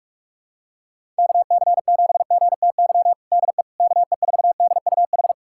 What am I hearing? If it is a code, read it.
Morse code, heard as a keyed tone, about 42 wpm, 690 Hz. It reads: KC7GTY DE KE4DUH
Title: USA Callsign DE Callsign - 6 Characters - 45wpm